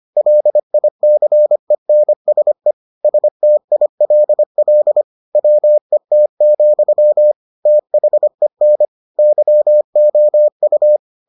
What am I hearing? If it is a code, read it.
Morse code, heard as a keyed tone, about 25 wpm, 600 Hz. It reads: LICENSE STILL WET, THEN YOU